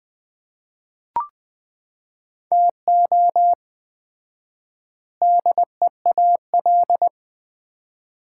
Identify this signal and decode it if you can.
Morse code, heard as a keyed tone, about 21 wpm, 700 Hz. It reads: TO DEAL
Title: 4x Word-Spacing - Sets of 4 Words - 20wpm